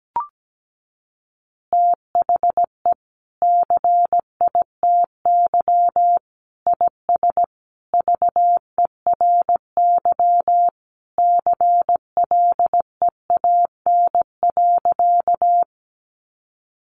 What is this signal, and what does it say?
Morse code, heard as a keyed tone, about 17 wpm, 700 Hz. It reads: THE CITY IS VERY CLEAN.
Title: Easy Phrases 17wpm